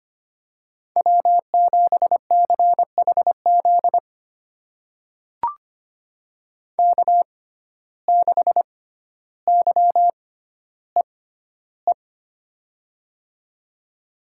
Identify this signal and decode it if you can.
Morse code, heard as a keyed tone, about 25 wpm, spaced as for 13 wpm, 700 Hz. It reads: W7CHZ K6YEE
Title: USA Call Signs - 5 Characters - 6x Character Spacing - 25wpm